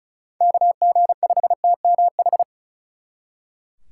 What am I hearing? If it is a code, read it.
Morse code, heard as a keyed tone, about 46 wpm, 700 Hz. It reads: KG5TMH